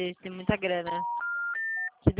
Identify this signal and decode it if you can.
Special information tone, with speech.
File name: out- -2004